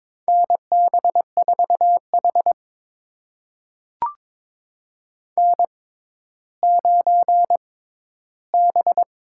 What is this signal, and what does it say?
Morse code, heard as a keyed tone, about 22 wpm, spaced as for 12 wpm, 700 Hz. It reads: NB4H N9B
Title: USA Call Signs - 4 Characters - 6x Character Spacing - 22wpm